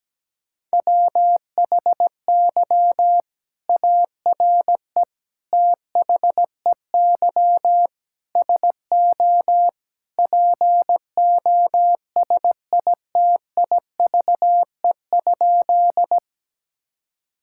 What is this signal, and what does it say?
Morse code, heard as a keyed tone, about 17 wpm, 700 Hz. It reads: WHY ARE THEY SO POSITIVE?
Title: Sentences from Top 1000 Words - Review Part 5 - 17wpm